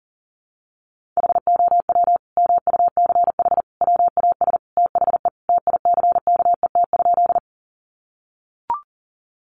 Morse code, heard as a keyed tone, about 37 wpm, 700 Hz. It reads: HOW MUCH WAS THE TICKET?
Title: Sentences from Top 1400 Words 40wpm